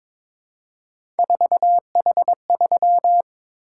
Morse code, heard as a keyed tone, about 22 wpm, 700 Hz. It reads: 4H3
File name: practice / 22wpm / Three Letter-Number Combinations - 22wpm